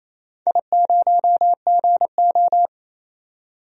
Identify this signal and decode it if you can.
Morse code, heard as a keyed tone, about 28 wpm, 700 Hz. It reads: I0GO
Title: DXCC-100 Call Signs - 4 Characters - 28wpm